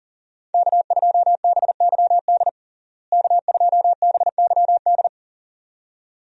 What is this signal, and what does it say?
Morse code, heard as a keyed tone, about 40 wpm, 700 Hz. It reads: K2BYD K2BYD
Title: USA Call Signs - 5 Characters - Repeated 2x - 40wpm